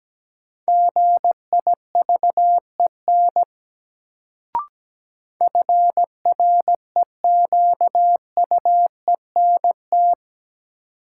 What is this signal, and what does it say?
Morse code, heard as a keyed tone, about 17 wpm, 700 Hz. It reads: GIVEN FREQUENT
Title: Sentences from Top 1400 Words - New Word Drill - 17wpm